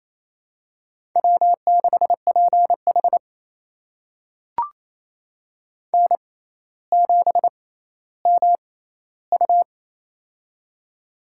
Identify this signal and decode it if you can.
Morse code, heard as a keyed tone, about 27 wpm, spaced as for 14 wpm, 700 Hz. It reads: W6PH N7MU
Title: USA Call Signs - 4 Characters - 6x Character Spacing - 28wpm